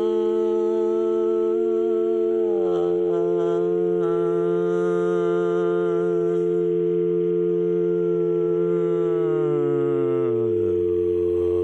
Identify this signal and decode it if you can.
Dial tone (North America), 350 + 440 Hz.